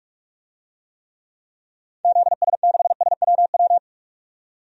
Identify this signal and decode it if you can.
Morse code, heard as a keyed tone, about 46 wpm, 700 Hz. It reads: ZS6SWW
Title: DXCC-100 Call Signs - 6 Characters - 45wpm